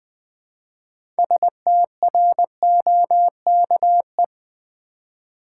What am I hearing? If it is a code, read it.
Morse code, heard as a keyed tone, about 20 wpm, 700 Hz. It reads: STROKE
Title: 6 Letter Words - First 3 Letters Given - 20wpm